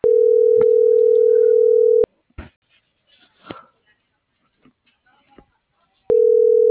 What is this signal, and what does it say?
Ringback tone (North America), 440 + 480 Hz.